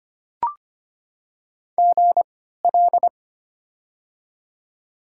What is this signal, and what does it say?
Morse code, heard as a keyed tone, about 25 wpm, spaced as for 14 wpm, 700 Hz. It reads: GL
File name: Top 90 QSO Elements - 3x Character Spacing - 25wpm